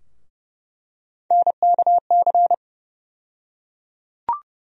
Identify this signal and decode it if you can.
Morse code, heard as a keyed tone, about 30 wpm, 700 Hz. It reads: NKC
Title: USA Call Sign Postfixes - 3 Letters 30wpm